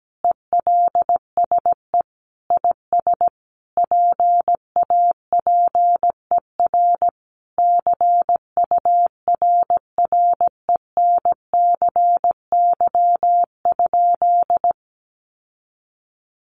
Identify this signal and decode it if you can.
Morse code, heard as a keyed tone, about 17 wpm, 700 Hz. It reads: ELSE IS PAPER CURRENCY?
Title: Sentences from Top 1500 Words 17wpm